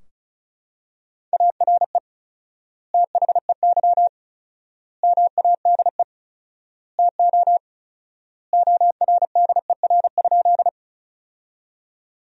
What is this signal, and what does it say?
Morse code, heard as a keyed tone, about 36 wpm, 700 Hz. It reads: ARE THEY MADE TO ORDER?
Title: 4x Word-Spacing - Sentences from Top 400 Words - 35wpm